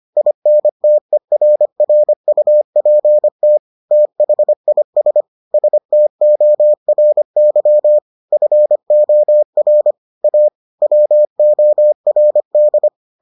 Morse code, heard as a keyed tone, about 25 wpm, 600 Hz. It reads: INTERRUPT THIS STORY FOR A WORD